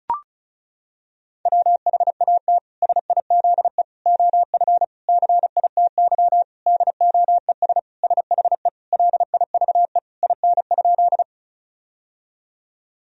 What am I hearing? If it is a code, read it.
Morse code, heard as a keyed tone, about 36 wpm, 700 Hz. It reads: WHAT SIZE OF CITY DOES SHE LIVE IN?
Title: Sentences from Top 1000 Words - Review Part 2 - 35wpm